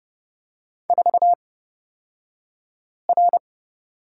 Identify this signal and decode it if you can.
Morse code, heard as a keyed tone, about 30 wpm, spaced as for 5 wpm, 700 Hz. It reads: 4R
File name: Lesson 38d - Letter q - Two Characters 30wpm